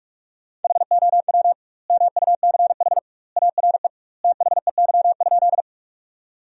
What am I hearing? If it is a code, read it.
Morse code, heard as a keyed tone, about 42 wpm, 700 Hz. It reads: HOW MUCH ARE THEY?
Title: 2x Word-Spacing - Sentences from Top 200 Words - 45wpm